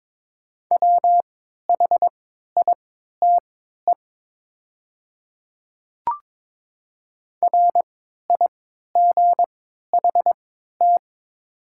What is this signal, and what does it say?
Morse code, heard as a keyed tone, about 22 wpm, spaced as for 12 wpm, 690 Hz. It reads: WHITE RIGHT
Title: 3x Character Spacing - Top 300 Words - 22wpm